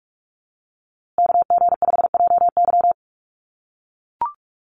Morse code, heard as a keyed tone, about 44 wpm, 700 Hz. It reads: KG5JY
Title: USA Call Signs Repeated 3x 45wpm